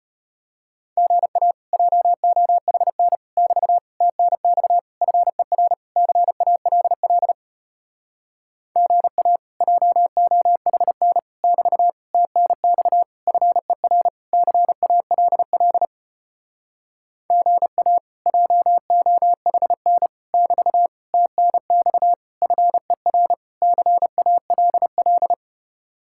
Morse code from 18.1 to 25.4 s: JOHN <BT> TNX FER CALL

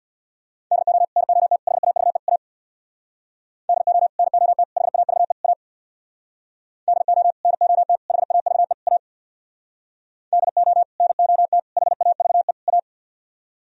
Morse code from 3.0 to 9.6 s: DO NOT HAVE A DO NOT HAVE A

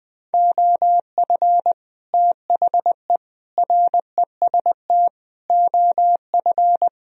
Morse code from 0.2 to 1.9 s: OF

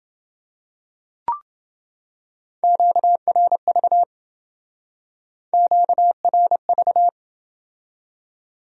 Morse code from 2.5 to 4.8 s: QRV

